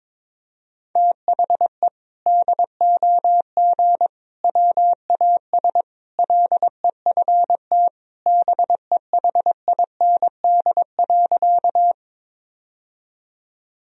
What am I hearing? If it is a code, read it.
Morse code, heard as a keyed tone, about 22 wpm, 700 Hz. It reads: THE DOG WAS LEFT BEHIND.
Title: Sentences from Top 1000 Words - Review Part 2 - 22wpm